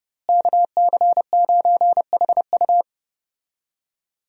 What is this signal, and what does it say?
Morse code, heard as a keyed tone, about 30 wpm, 700 Hz. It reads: KC9HU
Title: USA Call Signs - 5 Characters - Repeated 3x - 30wpm